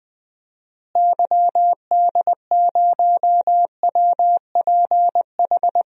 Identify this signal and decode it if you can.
Morse code, heard as a keyed tone, about 20 wpm, 700 Hz. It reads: YD0WPH
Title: DXCC-100 Call Signs - 6 Characters - Repeated 3x - 20wpm